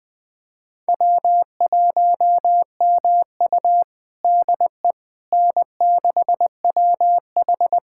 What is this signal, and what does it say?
Morse code, heard as a keyed tone, about 20 wpm, 700 Hz. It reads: W1MU DE N6WH
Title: USA Callsign DE Callsign - 4 Characters - 20wpm